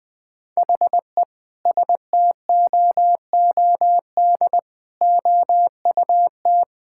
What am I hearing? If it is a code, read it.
Morse code, heard as a keyed tone, about 20 wpm, 700 Hz. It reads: HE STOOD OUT
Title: Sentences with 5-Word Phrases - Spoken Beginning - 20wpm